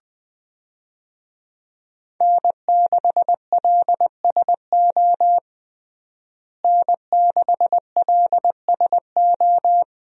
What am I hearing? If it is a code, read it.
Morse code, heard as a keyed tone, about 21 wpm, 700 Hz. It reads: N6LSO N6LSO
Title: USA Call Signs - 5 Characters - Repeated 2x - 20wpm